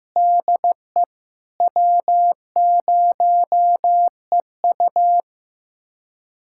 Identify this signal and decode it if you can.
Morse code, heard as a keyed tone, about 15 wpm, 700 Hz. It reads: DE W0EU
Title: USA Callsign DE Callsign - 4 Characters - 15wpm